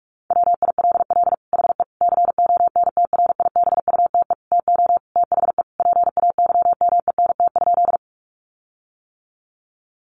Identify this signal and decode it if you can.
Morse code, heard as a keyed tone, about 44 wpm, 700 Hz. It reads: WILL HE CONTRIBUTE TO THE PAYMENT?
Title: Sentences from Top 1400 Words 45wpm